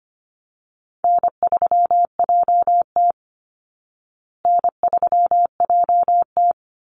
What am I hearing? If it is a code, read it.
Morse code, heard as a keyed tone, about 25 wpm, 700 Hz. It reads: N3JT N3JT
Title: Easy Call Signs Repeated 2x - 25wpm